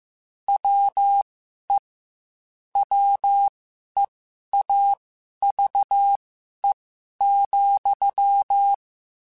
Morse code, heard as a keyed tone, about 15 wpm, spaced as for 11 wpm, 800 Hz. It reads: WE WEAVE,
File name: Quotes / Quotes-mp3s / Deceive @13